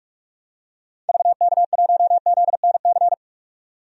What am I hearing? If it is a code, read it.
Morse code, heard as a keyed tone, about 46 wpm, 700 Hz. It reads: VK1ZNC
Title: DXCC-100 Call Signs - 6 Characters - 45wpm